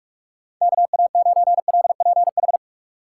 Morse code, heard as a keyed tone, about 46 wpm, 700 Hz. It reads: KA9LPH
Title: USA Call Signs - 6 Characters - 6x Character Spacing - 45wpm